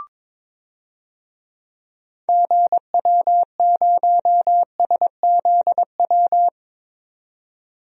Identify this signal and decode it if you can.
Morse code, heard as a keyed tone, about 21 wpm, 700 Hz. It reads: GW0SZW
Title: DXCC-100 Call Signs - 6 Characters - 22wpm